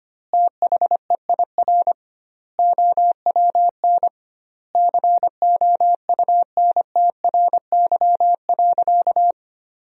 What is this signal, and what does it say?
Morse code, heard as a keyed tone, about 25 wpm, 700 Hz. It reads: THEIR OWN COUNTRY.
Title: 2x Word-Spacing - Sentences from Top 300 Words - 25wpm